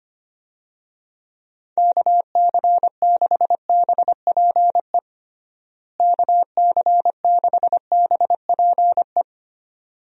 Morse code, heard as a keyed tone, about 25 wpm, 700 Hz. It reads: KC6BPE KC6BPE
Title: USA Call Signs - 6 Characters - Repeated 2x - 25wpm